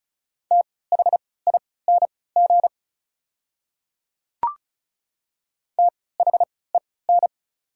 Morse code, heard as a keyed tone, about 36 wpm, spaced as for 20 wpm, 700 Hz. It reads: THING THEN
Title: 3x Character Spacing - Top 300 Words - 35wpm